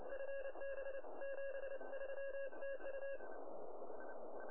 Morse code, heard as a keyed tone, about 29 wpm, 1650 Hz, with radio noise. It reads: FB 73 TU
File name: attachments / QSK